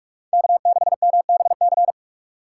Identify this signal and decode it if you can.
Morse code, heard as a keyed tone, about 45 wpm, 700 Hz. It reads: K6MBC